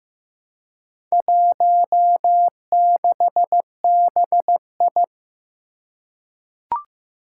Morse code, heard as a keyed tone, about 15 wpm, 700 Hz. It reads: 16BI
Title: Four Challenging Letter-Number Combinations 15wpm